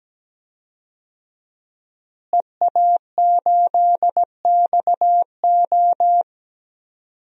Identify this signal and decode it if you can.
Morse code, heard as a keyed tone, about 17 wpm, 700 Hz. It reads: EA8XO
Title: DXCC-100 Call Signs - 5 Characters - Repeated 3x - 17wpm